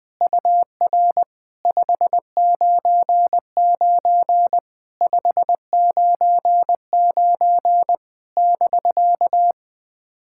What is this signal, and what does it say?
Morse code, heard as a keyed tone, about 20 wpm, 700 Hz. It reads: UR 599 599 <BK>
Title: POTA QSO Practice - 20wpm